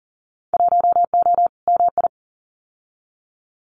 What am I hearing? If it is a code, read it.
Morse code, heard as a keyed tone, about 40 wpm, 700 Hz. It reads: 1O MI
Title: Winter Field Day Exchanges 40wpm